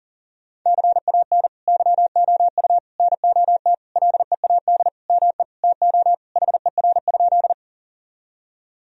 Morse code, heard as a keyed tone, about 40 wpm, 700 Hz. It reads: CAN YOU NOT LEAD ME TO HER?
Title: Sentences from Top 1000 Words - Review Part 2 - 40wpm